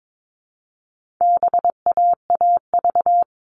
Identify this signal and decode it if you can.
Morse code, heard as a keyed tone, about 22 wpm, 700 Hz. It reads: BAAV
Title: All Cut Numbers - 4-Digit 22wpm